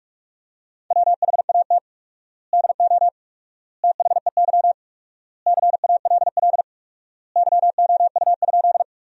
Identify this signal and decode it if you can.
Morse code, heard as a keyed tone, about 46 wpm, 700 Hz. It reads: WHAT DO THEY CALL YOU?
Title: 4x Word-Spacing - Sentences from Top 100 Words - 45wpm